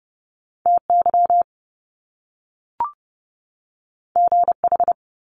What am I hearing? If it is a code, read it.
Morse code, heard as a keyed tone, about 30 wpm, 700 Hz. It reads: TY GH
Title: Top Two Letter Combinations in Words 30wpm